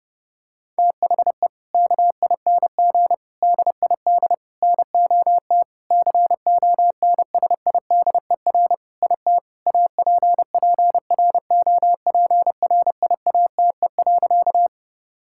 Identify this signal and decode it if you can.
Morse code, heard as a keyed tone, about 30 wpm, 700 Hz. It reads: THE KING DID NOT CONSIDER IT APPROPRIATE.
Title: Sentences from Top 1000 Words - Review Part 5 - 30wpm